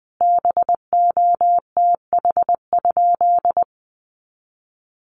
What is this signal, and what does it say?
Morse code, heard as a keyed tone, about 20 wpm, 700 Hz. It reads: BOTH?
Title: Sentences from Top 1200 Words 20wpm